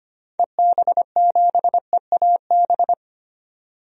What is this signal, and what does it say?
Morse code, heard as a keyed tone, about 25 wpm, 700 Hz. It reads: EB7EAB
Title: DXCC-100 Call Signs - 6 Characters - 25wpm